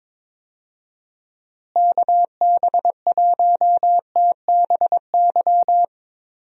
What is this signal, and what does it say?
Morse code, heard as a keyed tone, about 22 wpm, 700 Hz. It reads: KB1TBY
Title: USA Call Signs - 6 Characters - Repeated 3x - 22wpm